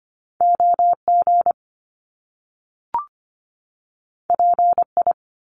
Morse code, heard as a keyed tone, about 25 wpm, 700 Hz. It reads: OG PI
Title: Top Two Letter Combinations in Words 25wpm